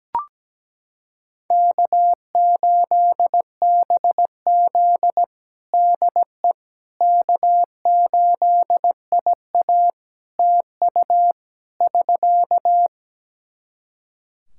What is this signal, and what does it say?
Morse code, heard as a keyed tone, about 17 wpm, 700 Hz. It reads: K8BZ DE K8IA TU <SK>
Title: K1USN SST QSO Practice 17wpm